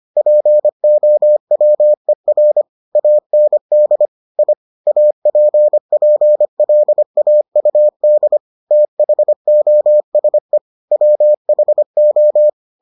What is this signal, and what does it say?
Morse code, heard as a keyed tone, about 25 wpm, 600 Hz. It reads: POWER AND I APPLAUD THOSE WHO